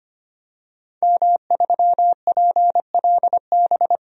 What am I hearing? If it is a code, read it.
Morse code, heard as a keyed tone, about 25 wpm, 700 Hz. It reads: M3PLB
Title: DXCC-100 Call Signs - 5 Characters - 25wpm